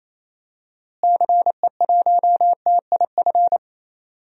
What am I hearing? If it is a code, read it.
Morse code, heard as a keyed tone, about 28 wpm, 700 Hz. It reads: CE1TIF
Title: DXCC-100 Call Signs - 6 Characters - Repeated 2x - 28wpm